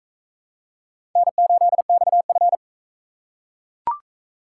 Morse code, heard as a keyed tone, about 41 wpm, 700 Hz. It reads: N8XF